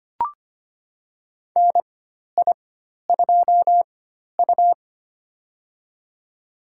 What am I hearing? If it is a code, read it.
Morse code, heard as a keyed tone, about 25 wpm, spaced as for 12 wpm, 700 Hz. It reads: NI2U
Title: USA Call Signs - 4 Characters - 4x Character Spacing - 25wpm